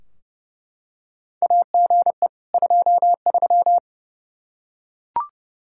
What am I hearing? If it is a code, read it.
Morse code, heard as a keyed tone, about 30 wpm, 700 Hz. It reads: AGE 23
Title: Practice QSO - Exchange 3 - More Info - 30wpm